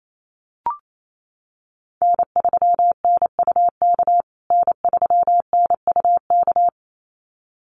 Morse code, heard as a keyed tone, about 28 wpm, 700 Hz. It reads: N3NUK N3NUK
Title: USA Call Signs Repeated 2x 28wpm